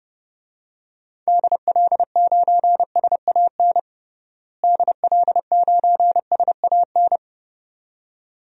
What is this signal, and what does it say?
Morse code, heard as a keyed tone, about 30 wpm, 700 Hz. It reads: DL9SAN DL9SAN
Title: DXCC-100 Call Signs - 6 Characters - Repeated 2x - 30wpm